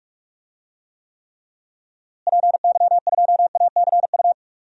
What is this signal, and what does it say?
Morse code, heard as a keyed tone, about 42 wpm, 700 Hz. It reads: PY2ACU
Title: DXCC-100 Call Signs - 6 Characters - Repeated 3x - 45wpm